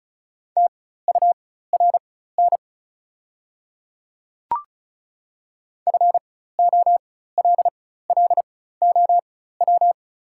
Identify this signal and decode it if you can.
Morse code, heard as a keyed tone, about 34 wpm, spaced as for 16 wpm, 700 Hz. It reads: TURN FOLLOW
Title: 4x Character Spacing - Top 200 Words - 35wpm